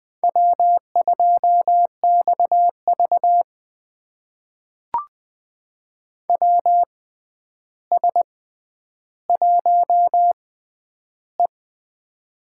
Morse code, heard as a keyed tone, about 20 wpm, spaced as for 10 wpm, 700 Hz. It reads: W2XV WS1E
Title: USA Call Signs - 4 Characters - 6x Character Spacing - 20wpm